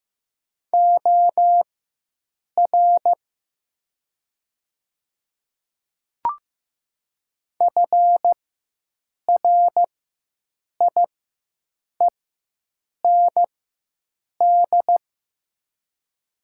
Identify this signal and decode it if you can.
Morse code, heard as a keyed tone, about 15 wpm, spaced as for 7 wpm, 700 Hz. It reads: OR FRIEND
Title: 4x Character Spacing - Top 300 Words - 15wpm